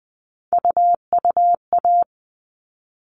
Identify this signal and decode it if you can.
Morse code, heard as a keyed tone, about 20 wpm, 700 Hz. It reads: UUA
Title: All Cut Numbers - 3-Digit 20wpm